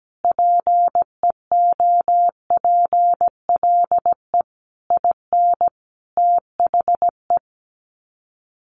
Morse code, heard as a keyed tone, about 17 wpm, 700 Hz. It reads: PEOPLE IN THE
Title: Sets of 5 Words Encore 17wpm